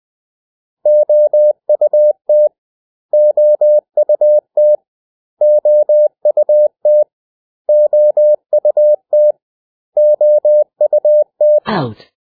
Morse code, heard as a keyed tone, about 20 wpm, 600 Hz. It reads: OUT OUT OUT OUT OUT
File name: cw / 100 Common English Words